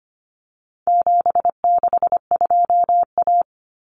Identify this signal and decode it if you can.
Morse code, heard as a keyed tone, about 25 wpm, 700 Hz. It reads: 762A
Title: Common Cut Numbers - 4-Digit 25wpm